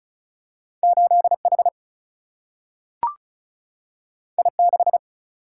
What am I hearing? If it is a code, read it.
Morse code, heard as a keyed tone, about 36 wpm, 700 Hz. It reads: 8 H I 6
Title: Single-Letter Single-Number 35wpm